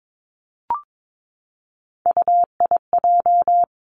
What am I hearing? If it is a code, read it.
Morse code, heard as a keyed tone, about 22 wpm, 700 Hz. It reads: UIJ
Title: Three Challenging Letter-Number Combinations 22wpm